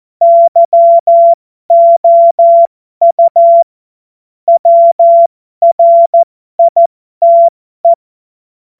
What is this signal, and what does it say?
Morse code, heard as a keyed tone, about 14 wpm, spaced as for 12 wpm, 680 Hz. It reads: YOU WRITE
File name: Morsekurs / Witze